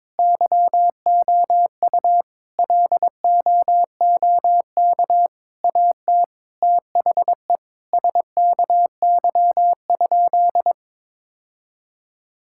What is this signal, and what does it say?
Morse code, heard as a keyed tone, about 22 wpm, 700 Hz. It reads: YOU LOOK AT THE SKY?
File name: practice / 22wpm / Sentences with 4-Word Phrases - 22wpm